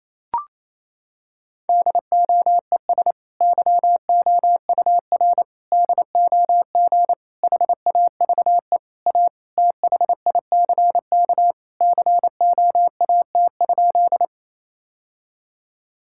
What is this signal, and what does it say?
Morse code, heard as a keyed tone, about 28 wpm, 700 Hz. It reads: DOES YOUR DOG HAVE A THICK COAT?
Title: Sentences from Top 1600 Words 28wpm